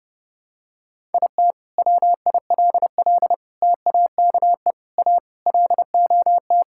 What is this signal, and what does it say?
Morse code, heard as a keyed tone, about 30 wpm, 700 Hz. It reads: IT WILL TAKE A LOT
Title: Sentences from Top 1000 Words - Review Part 6 - 30wpm